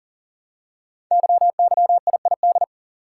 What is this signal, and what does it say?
Morse code, heard as a keyed tone, about 39 wpm, 700 Hz. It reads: YYIID
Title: Code Groups - 5 Letter:Numbers - 40wpm